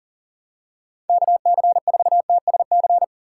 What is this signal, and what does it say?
Morse code, heard as a keyed tone, about 37 wpm, 700 Hz. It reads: KC4TSC
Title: USA Call Signs - 6 Characters - 2x Character Spacing - 40wpm